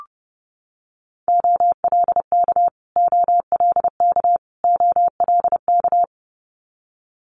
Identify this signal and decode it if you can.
Morse code, heard as a keyed tone, about 30 wpm, 700 Hz. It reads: OLK OLK OLK
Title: Multi-letter Phonemes Repeated 3x 30wpm